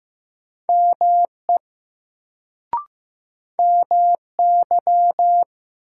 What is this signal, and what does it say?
Morse code, heard as a keyed tone, about 15 wpm, 700 Hz. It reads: ME MY
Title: Sets of 2 Words - New Word Drill - 15wpm